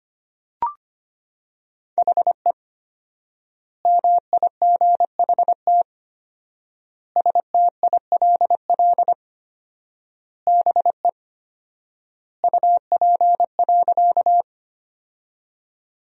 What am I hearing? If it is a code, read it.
Morse code, heard as a keyed tone, about 25 wpm, 700 Hz. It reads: HE MIGHT STILL BE UP.
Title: 4x Word-Spacing - Sentences from Top 300 Words - 25wpm